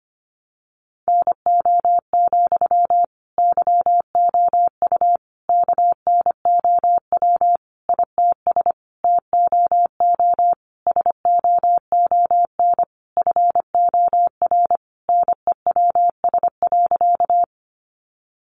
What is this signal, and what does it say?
Morse code, heard as a keyed tone, about 25 wpm, 700 Hz. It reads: NO, YOU KNOW ITS TOO SOON FOR NEWS.
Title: Sentences from Top 800 Words 25wpm